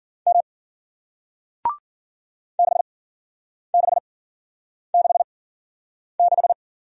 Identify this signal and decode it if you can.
Morse code, heard as a keyed tone, about 48 wpm, spaced as for 9 wpm, 700 Hz. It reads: M 6666
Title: Speed-Racing Single Letter-Number 40wpm